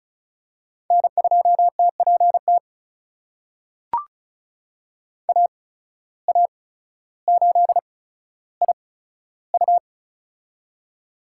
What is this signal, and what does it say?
Morse code, heard as a keyed tone, about 36 wpm, spaced as for 15 wpm, 700 Hz. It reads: N2TPT AA8IU
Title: USA Call Signs - 5 Characters - 8x Character Spacing - 35wpm